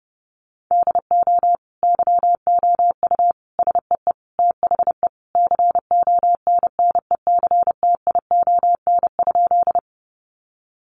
Morse code, heard as a keyed tone, about 30 wpm, 700 Hz. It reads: DO YOU SEE THE CONNECTION?
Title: Sentences from Top 1300 Words 30wpm